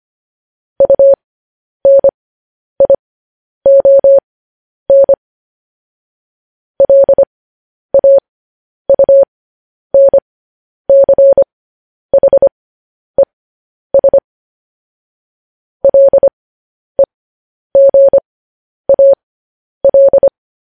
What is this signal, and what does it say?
Morse code, heard as a keyed tone, about 25 wpm, spaced as for 10 wpm, 550 Hz. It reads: UNION LAUNCHES LEGAL
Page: news-headlines-in-morse-code-at-10-wpm